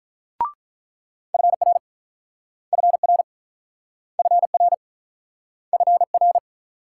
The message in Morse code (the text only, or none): FR FR FR FR